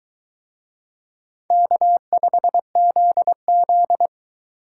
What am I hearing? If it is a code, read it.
Morse code, heard as a keyed tone, about 22 wpm, 700 Hz. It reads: K5ZZ